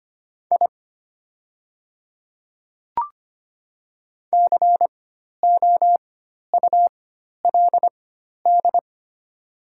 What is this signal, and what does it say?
Morse code, heard as a keyed tone, about 25 wpm, spaced as for 12 wpm, 690 Hz. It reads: I COULD
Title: 4x Character Spacing - Top 100 Words - 25wpm